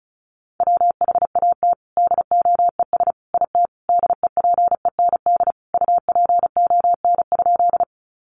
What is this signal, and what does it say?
Morse code, heard as a keyed tone, about 34 wpm, 700 Hz. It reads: WHAT DOES IT DEPEND UPON?
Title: Sentences from Top 1300 Words 35wpm